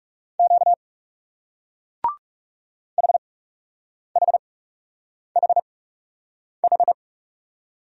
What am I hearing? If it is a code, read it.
Morse code, heard as a keyed tone, about 40 wpm, spaced as for 8 wpm, 700 Hz. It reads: Q HHHH